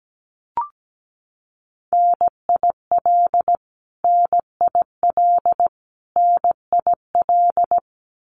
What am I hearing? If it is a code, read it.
Morse code, heard as a keyed tone, about 17 wpm, 700 Hz. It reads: NIL NIL NIL
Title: Abbreviations Repeated 3x - 17wpm